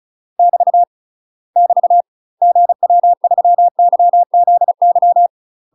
Morse code, heard as a keyed tone, about 36 wpm, 700 Hz. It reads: <BT> <BT> GW3YZY
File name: UK MASTER QSOs TRAINING / 35WPM / UK MASTER QSOs 0027